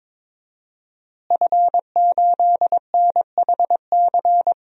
Morse code, heard as a keyed tone, about 22 wpm, 700 Hz. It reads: F8NHC
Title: DXCC-100 Call Signs - 5 Characters - Repeated 3x - 22wpm